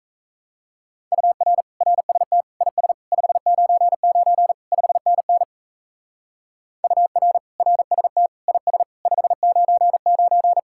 Morse code, heard as a keyed tone, about 34 wpm, 700 Hz. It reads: UR RST IS 599 5NN UR RST IS 599
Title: Speed-Racing Common QSO Phrases - 28wpm